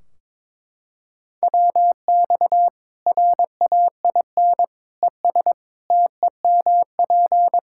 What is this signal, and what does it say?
Morse code, heard as a keyed tone, about 23 wpm, 700 Hz. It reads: WX RAIN ES TEMP
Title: Practice QSO - Exchange 2 - Info - 22wpm